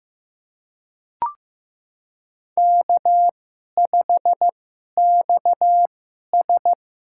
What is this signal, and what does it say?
Morse code, heard as a keyed tone, about 15 wpm, spaced as for 11 wpm, 700 Hz. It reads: K5XS
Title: USA Call Signs - 4 Characters - 2x Character Spacing - 15wpm